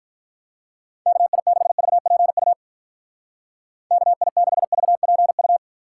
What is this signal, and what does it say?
Morse code, heard as a keyed tone, about 45 wpm, 700 Hz. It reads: KI6VPU KI6VPU